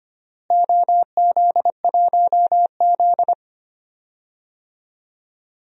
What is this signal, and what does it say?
Morse code, heard as a keyed tone, about 25 wpm, 700 Hz. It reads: OZ1Z
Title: DXCC-100 Call Signs - 4 Characters - Repeated 2x - 25wpm